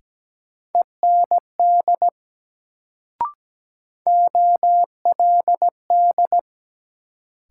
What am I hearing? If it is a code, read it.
Morse code, heard as a keyed tone, about 17 wpm, 700 Hz. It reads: END OLD
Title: Sentences from Top 200 Words - New Word Drill - 17wpm